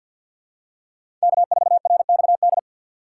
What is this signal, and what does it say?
Morse code, heard as a keyed tone, about 51 wpm, 700 Hz. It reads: K4RXD
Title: USA Call Signs - 5 Characters - 2x Character Spacing - 50wpm